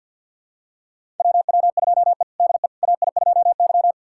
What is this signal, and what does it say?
Morse code, heard as a keyed tone, about 47 wpm, 690 Hz. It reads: WW2E DE AI2Y
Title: USA Callsign DE Callsign - 4 Characters - 50wpm